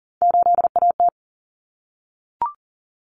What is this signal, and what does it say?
Morse code, heard as a keyed tone, about 40 wpm, 700 Hz. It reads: 8AT
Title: Common Cut Numbers - 3-Digit 40wpm